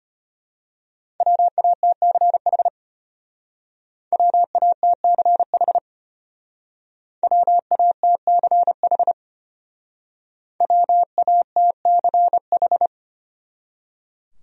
Morse code, heard as a keyed tone, about 32 wpm, 700 Hz. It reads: WATCH WATCH WATCH WATCH